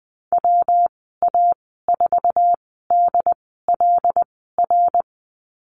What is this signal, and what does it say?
Morse code, heard as a keyed tone, about 20 wpm, spaced as for 15 wpm, 700 Hz. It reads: WA4DLR
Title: 173 Lesson 28g - Number 9 - Call Signs 20wpm